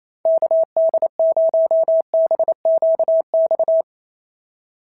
Morse code, heard as a keyed tone, about 28 wpm, 690 Hz. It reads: KD0BQX
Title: SOTA QSO Practice 28wpm